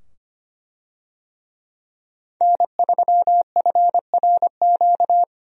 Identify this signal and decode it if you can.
Morse code, heard as a keyed tone, about 25 wpm, 700 Hz. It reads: N3FRQ
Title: USA Call Signs - 5 Characters - 25wpm